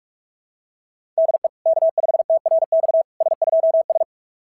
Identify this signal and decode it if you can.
Morse code, heard as a keyed tone, about 43 wpm, 660 Hz. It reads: DE K5TRX S2S
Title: SOTA QSO Practice 45wpm